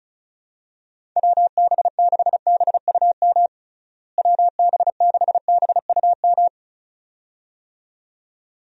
Morse code, heard as a keyed tone, about 36 wpm, 700 Hz. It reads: WB6BUM WB6BUM
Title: USA Call Signs - 6 Characters - Repeated 2x - 35wpm